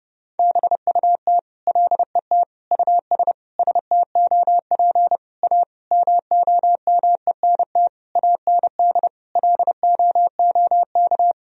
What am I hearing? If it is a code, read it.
Morse code, heard as a keyed tone, about 30 wpm, 700 Hz. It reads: BUT LET US STOP A MOMENT AND LOOK
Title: Sentences from Top 1000 Words - Review Part 1 - 30wpm